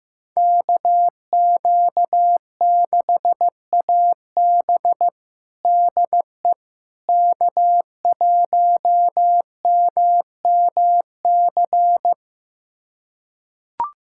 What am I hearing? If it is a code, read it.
Morse code, heard as a keyed tone, about 15 wpm, 700 Hz. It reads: KQ6AB DE K1MMC
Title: USA Callsign DE Callsign - 5 Characters - 15wpm